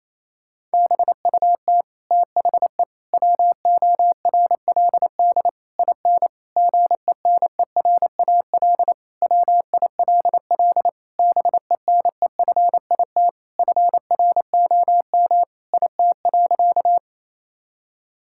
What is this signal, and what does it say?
Morse code, heard as a keyed tone, about 28 wpm, 700 Hz. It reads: BUT THE WORLD IN GENERAL WILL BENEFIT FROM IT.
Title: Sentences from Top 1000 Words - Review Part 6 - 28wpm